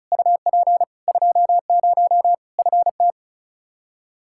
Morse code, heard as a keyed tone, about 36 wpm, 700 Hz. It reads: UP 20 FT
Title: Practice QSO - Exchange 2 - Info - 35wpm